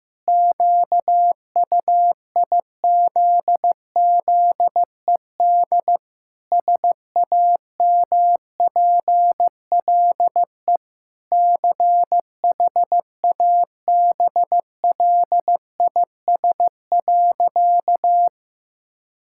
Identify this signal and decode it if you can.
Morse code, heard as a keyed tone, about 15 wpm, 700 Hz. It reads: QUIZZED SAMPLE CHABLIS.
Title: Pangram Challenge v2 - 15wpm